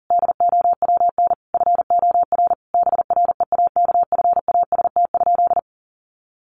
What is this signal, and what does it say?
Morse code, heard as a keyed tone, about 37 wpm, 700 Hz. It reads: DOWN FOR BREAKFAST?
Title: Sentences from Top 1500 Words 40wpm